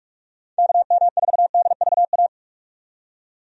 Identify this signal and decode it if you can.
Morse code, heard as a keyed tone, about 41 wpm, 700 Hz. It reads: KM4DVA
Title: USA Call Signs - 6 Characters - 45wpm